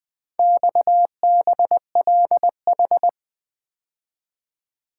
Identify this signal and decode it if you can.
Morse code, heard as a keyed tone, about 24 wpm, 700 Hz. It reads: XBLH
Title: Speed-Racing - Four Letter-Number Combinations - 20wpm